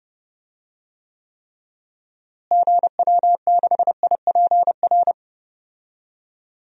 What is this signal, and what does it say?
Morse code, heard as a keyed tone, about 32 wpm, 700 Hz. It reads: GW6IPR
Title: DXCC-100 Call Signs - 6 Characters - Repeated 3x - 30wpm